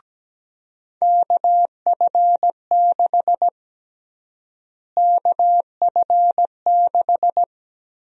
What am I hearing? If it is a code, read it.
Morse code, heard as a keyed tone, about 17 wpm, 700 Hz. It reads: KF6 KF6
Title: USA Call Sign Prefixes - Repeated 2x - 17wpm